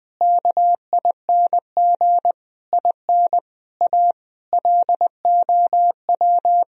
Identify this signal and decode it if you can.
Morse code, heard as a keyed tone, about 21 wpm, 700 Hz. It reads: KING IN A LOW
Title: Sentences from Top 1000 Words - Review Part 1 - 20wpm